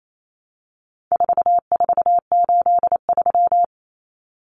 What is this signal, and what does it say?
Morse code, heard as a keyed tone, about 28 wpm, 700 Hz. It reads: 4483
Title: Common Cut Numbers - 4-Digit 28wpm